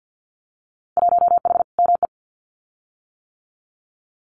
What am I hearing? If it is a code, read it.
Morse code, heard as a keyed tone, about 50 wpm, 700 Hz. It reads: 1H ME
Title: Winter Field Day Exchanges 50wpm